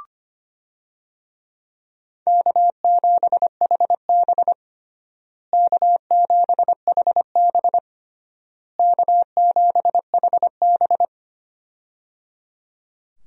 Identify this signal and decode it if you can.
Morse code, heard as a keyed tone, about 25 wpm, 700 Hz. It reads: K7HB K7HB K7HB
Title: USA Call Signs - 4 Characters - Repeated 3x - 25wpm